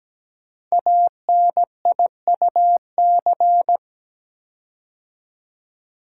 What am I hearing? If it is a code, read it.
Morse code, heard as a keyed tone, about 17 wpm, 700 Hz. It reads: ANIUC
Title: Code Groups - 5 Letter:Numbers - 17wpm